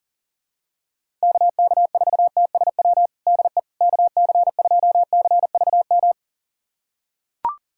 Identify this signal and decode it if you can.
Morse code, heard as a keyed tone, about 40 wpm, 700 Hz. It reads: KK4TSW DE KC2CVM
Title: USA Callsign DE Callsign - 6 Characters - 40wpm